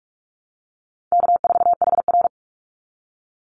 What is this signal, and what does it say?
Morse code, heard as a keyed tone, about 46 wpm, 700 Hz. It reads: K4HR
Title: Easy Call Signs - 45wpm